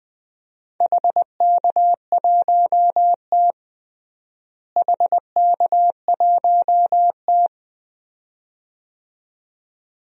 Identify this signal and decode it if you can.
Morse code, heard as a keyed tone, about 20 wpm, 700 Hz. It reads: HK1T HK1T
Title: DXCC-100 Call Signs - 4 Characters - Repeated 2x - 20wpm